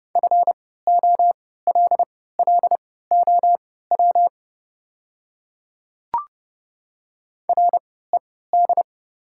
Morse code, heard as a keyed tone, about 30 wpm, spaced as for 17 wpm, 700 Hz. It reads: FOLLOW RED
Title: 3x Character Spacing - Top 500 Words - 30wpm